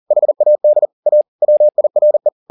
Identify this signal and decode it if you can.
Morse code, heard as a keyed tone, about 40 wpm, 600 Hz. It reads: HAD A WIRE